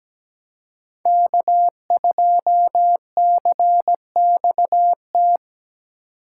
Morse code, heard as a keyed tone, about 20 wpm, 700 Hz. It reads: K2CXT
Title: Speed Racing - USA 1x3 Call Signs 17wpm